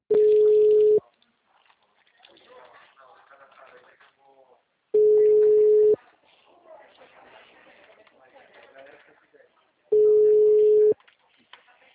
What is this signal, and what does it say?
Ringback tone (Europe, 425 Hz), with speech.